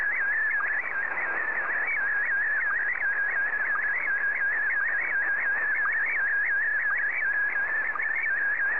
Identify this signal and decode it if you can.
SSTV (slow-scan television), Scottie DX.
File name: sstv / SDX